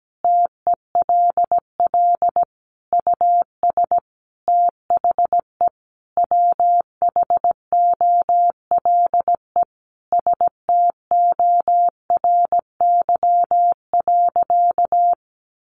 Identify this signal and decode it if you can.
Morse code, heard as a keyed tone, about 17 wpm, 700 Hz. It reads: TELL US THE WHOLE STORY.
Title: Sentences from Top 1100 Words 17wpm